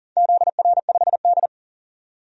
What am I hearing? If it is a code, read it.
Morse code, heard as a keyed tone, about 40 wpm, 700 Hz. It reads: ZR5D
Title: DXCC-100 Call Signs - 4 Characters - 40wpm